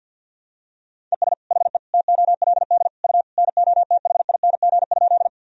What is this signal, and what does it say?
Morse code, heard as a keyed tone, about 52 wpm, 700 Hz. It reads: ES HE TOLD U NOTHING?
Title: Sentences from Top 500 Words and Abbreviations 50wpm